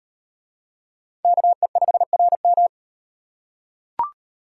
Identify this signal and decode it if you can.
Morse code, heard as a keyed tone, about 36 wpm, 700 Hz. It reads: KE5RM